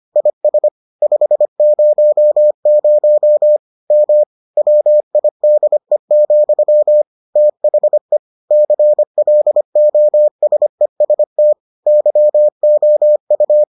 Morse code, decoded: IS 500 M WIDE, THE CLOSEST YOU

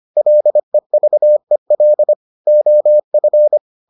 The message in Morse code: LEVEL OF

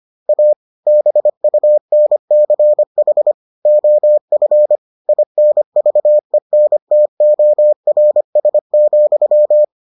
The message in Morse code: A BUNCH OF INVENTORS,

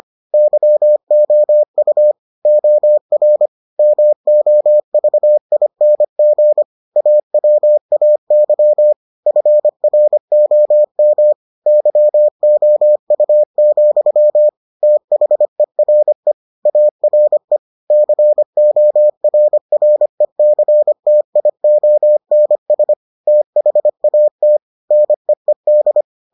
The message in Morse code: YOU OR MOVING AWAY FROM YOU, THERE ARE CORRECTIONS THAT NEED

